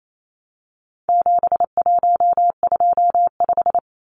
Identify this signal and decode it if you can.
Morse code, heard as a keyed tone, about 28 wpm, 700 Hz. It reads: 7125
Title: Numbers - 4-Digit 28wpm